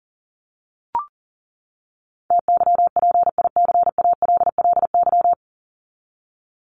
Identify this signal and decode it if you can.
Morse code, heard as a keyed tone, about 40 wpm, 700 Hz. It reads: TYPICALLY
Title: LY Suffix - Top 100 Words with Base Words 40wpm